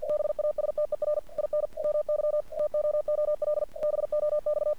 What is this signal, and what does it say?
Morse code, heard as a keyed tone, about 49 wpm, 600 Hz. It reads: BASTER IN MY TOOL BOX
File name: archived / CRSnet